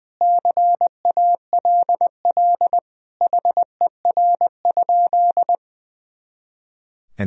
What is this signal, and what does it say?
Morse code, heard as a keyed tone, about 20 wpm, 700 Hz. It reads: CALL HER?
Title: Sentences from Top 100 Words and Abbreviations 20wpm